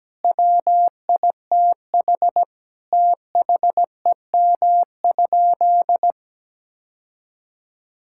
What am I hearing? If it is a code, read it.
Morse code, heard as a keyed tone, about 17 wpm, 700 Hz. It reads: WITH THEM?
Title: Sentences from Top 1000 Words - Review Part 5 - 17wpm